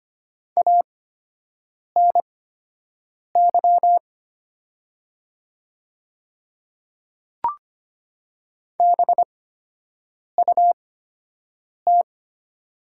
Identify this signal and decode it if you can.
Morse code, heard as a keyed tone, about 25 wpm, spaced as for 7 wpm, 700 Hz. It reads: ANY BUT
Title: 8x Character Spacing - Top 200 Words - 25wpm